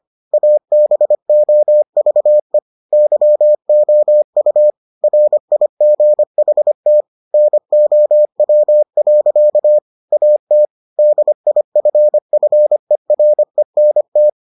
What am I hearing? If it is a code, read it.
Morse code, heard as a keyed tone, about 25 wpm, 600 Hz. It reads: ABOVE YOU RIGHT NOW. AT DIFFERENT